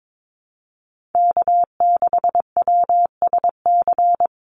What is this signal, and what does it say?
Morse code, heard as a keyed tone, about 22 wpm, 700 Hz. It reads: K6WSC
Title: USA Call Signs Repeated 2x 22wpm